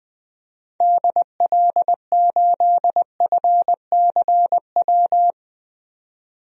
Morse code, heard as a keyed tone, about 20 wpm, 700 Hz. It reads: DL8FCW
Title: DXCC-100 Call Signs - 6 Characters - 20wpm